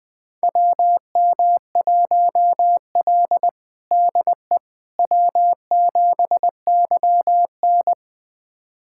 Morse code, heard as a keyed tone, about 20 wpm, 700 Hz. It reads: WM1L DE W7YN
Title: USA Callsign DE Callsign - 4 Characters - 20wpm